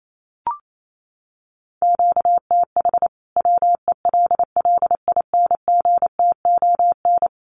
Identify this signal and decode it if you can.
Morse code, heard as a keyed tone, about 28 wpm, 700 Hz. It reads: QTH WELLINGTON
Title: Common QSO Phrases 28wpm